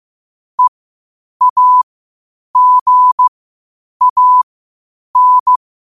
Morse code, heard as a keyed tone, about 15 wpm, spaced as for 8 wpm, 1000 Hz. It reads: EAGAN
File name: qotdfiles / 8wpm